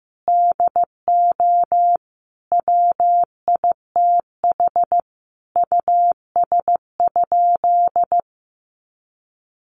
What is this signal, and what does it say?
Morse code, heard as a keyed tone, about 15 wpm, 700 Hz. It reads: DO WITH US?
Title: Sentences from Top 200 Words 15wpm